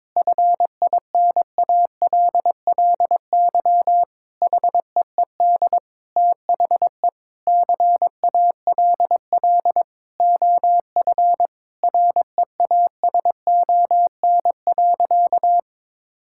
Morse code, heard as a keyed tone, about 21 wpm, 700 Hz. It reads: FINALLY HEED THE CALL OF REASON.
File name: practice / 22wpm / Sentences with 8-Word Phrases - 22wpm